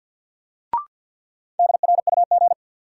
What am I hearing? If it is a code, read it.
Morse code, heard as a keyed tone, about 47 wpm, 700 Hz. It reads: DRUG
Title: Sets of 5 Words - New Word Drill - 50wpm